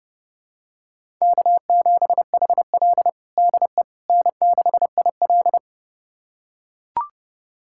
Morse code, heard as a keyed tone, about 30 wpm, 700 Hz. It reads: K7HL DE N6IL
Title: USA Callsign DE Callsign - 4 Characters - 30wpm